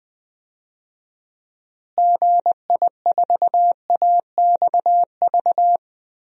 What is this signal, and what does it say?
Morse code, heard as a keyed tone, about 20 wpm, 700 Hz. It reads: GI4AXV
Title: DXCC-100 Call Signs - 6 Characters - Repeated 3x - 20wpm